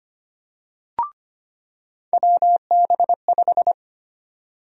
Morse code, heard as a keyed tone, about 25 wpm, 700 Hz. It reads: WB5
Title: USA Call Sign Prefixes - 25wpm